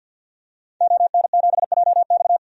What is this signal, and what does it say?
Morse code, heard as a keyed tone, about 52 wpm, 700 Hz. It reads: ON7JX
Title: DXCC-100 Call Signs - 5 Characters - Repeated 3x - 50wpm